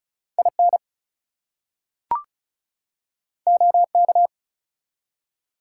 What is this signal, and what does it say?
Morse code, heard as a keyed tone, about 33 wpm, 700 Hz. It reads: IN OK
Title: Two Letter Words - 35wpm